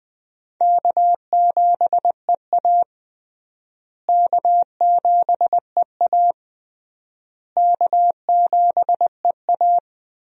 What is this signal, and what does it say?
Morse code, heard as a keyed tone, about 21 wpm, 700 Hz. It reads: K7EA K7EA K7EA
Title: USA Call Signs - 4 Characters - Repeated 3x - 20wpm